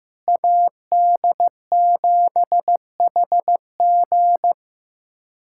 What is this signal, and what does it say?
Morse code, heard as a keyed tone, about 15 wpm, 700 Hz. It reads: AD7HG
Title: USA Callsign DE Callsign - 5 Characters - 15wpm